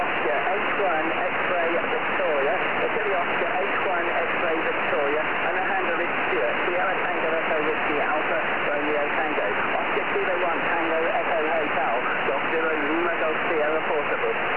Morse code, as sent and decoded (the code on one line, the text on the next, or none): none